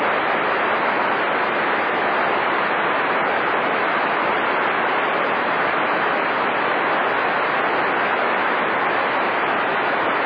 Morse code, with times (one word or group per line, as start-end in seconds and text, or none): none